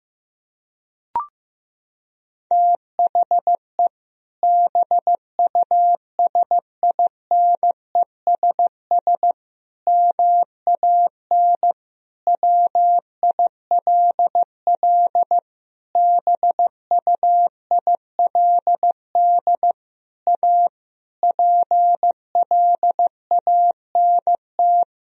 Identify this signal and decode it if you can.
Morse code, heard as a keyed tone, about 16 wpm, 700 Hz. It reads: THE BUSINESS MAN WILL BUILD A PLANT
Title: Sentences from Top 1000 Words - Review Part 1 - 15wpm